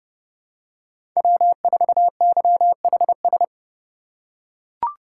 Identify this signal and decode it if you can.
Morse code, heard as a keyed tone, about 29 wpm, 700 Hz. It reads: W4YHS
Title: USA Call Signs - 2x Character Spacing - 30wpm